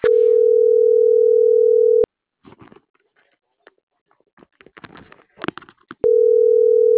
Ringback tone (North America), 440 + 480 Hz.